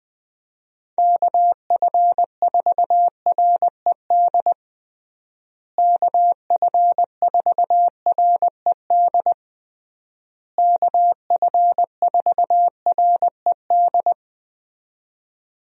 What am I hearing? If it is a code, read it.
Morse code, heard as a keyed tone, about 21 wpm, 700 Hz. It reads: KF4RED KF4RED KF4RED
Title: USA Call Signs - 6 Characters - Repeated 3x - 20wpm